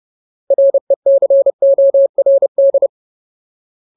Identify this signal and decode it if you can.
Morse code, heard as a keyed tone, about 30 wpm, 550 Hz. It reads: RECORD
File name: Ultimate Word List / 30WPM Ultimate